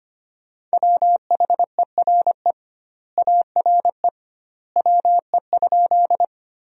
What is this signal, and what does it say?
Morse code, heard as a keyed tone, about 25 wpm, 700 Hz. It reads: WHERE ARE WE?
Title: 2x Word-Spacing - Sentences from Top 200 Words - 25wpm